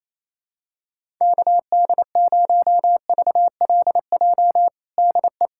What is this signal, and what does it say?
Morse code, heard as a keyed tone, about 28 wpm, 700 Hz. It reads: KD0VLJ DE
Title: USA Callsign DE Callsign - 6 Characters - 28wpm